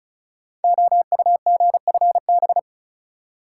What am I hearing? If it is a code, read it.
Morse code, heard as a keyed tone, about 36 wpm, 700 Hz. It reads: OUGFB
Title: Code Groups - 5 Letters - 35wpm